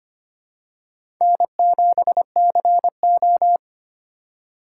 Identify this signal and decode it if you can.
Morse code, heard as a keyed tone, about 25 wpm, 700 Hz. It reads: N7CO
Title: USA Call Signs - 4 Characters - 2x Character Spacing - 25wpm